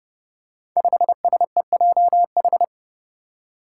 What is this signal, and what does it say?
Morse code, heard as a keyed tone, about 30 wpm, 700 Hz. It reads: 5SEJH
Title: Five Challenging Letter-Number Combinations 30wpm